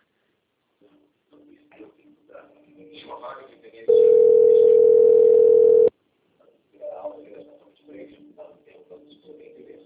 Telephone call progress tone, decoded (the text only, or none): ringback tone